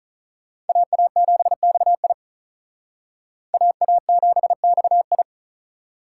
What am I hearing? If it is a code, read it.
Morse code, heard as a keyed tone, about 45 wpm, 700 Hz. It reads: AA7XI AA7XI